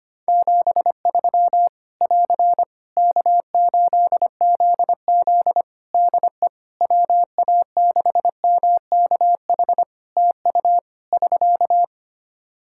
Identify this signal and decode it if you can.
Morse code, heard as a keyed tone, about 25 wpm, 700 Hz. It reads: 73 <AR> K8ZZ DE WA6MKH TU <SK>
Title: K1USN SST QSO Practice 25wpm